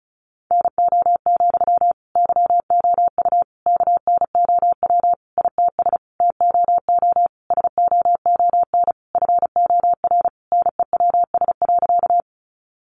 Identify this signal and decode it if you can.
Morse code, heard as a keyed tone, about 34 wpm, 700 Hz. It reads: NO, YOU KNOW ITS TOO SOON FOR NEWS.
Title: Sentences from Top 800 Words 35wpm